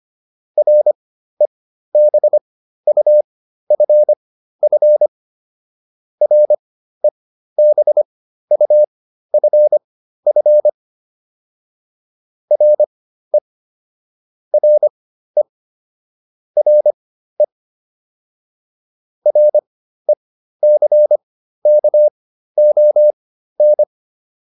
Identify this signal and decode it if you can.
Morse code, heard as a keyed tone, about 25 wpm, spaced as for 13 wpm, 600 Hz. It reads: REBUFF REBUFF RE RE RE RECKON